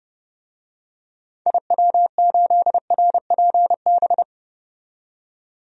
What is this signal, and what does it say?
Morse code, heard as a keyed tone, about 30 wpm, 700 Hz. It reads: IW8RPB
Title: DXCC-100 Call Signs - 6 Characters - 30wpm